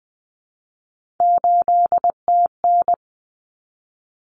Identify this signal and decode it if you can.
Morse code, heard as a keyed tone, about 20 wpm, 700 Hz. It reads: 8TN
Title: Common Cut Numbers - 3-Digit 20wpm